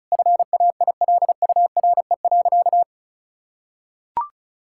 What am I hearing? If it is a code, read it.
Morse code, heard as a keyed tone, about 34 wpm, 700 Hz. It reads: FAILURE.
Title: Sentences from Top 1000 Words - Review Part 6 - 35wpm